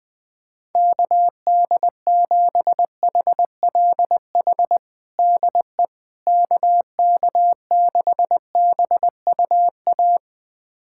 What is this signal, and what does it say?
Morse code, heard as a keyed tone, about 21 wpm, 700 Hz. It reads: KD7HLH DE KK6BUA
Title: USA Callsign DE Callsign - 6 Characters - 20wpm